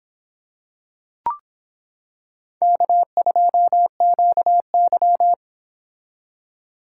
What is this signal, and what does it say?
Morse code, heard as a keyed tone, about 25 wpm, 700 Hz. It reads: K2QY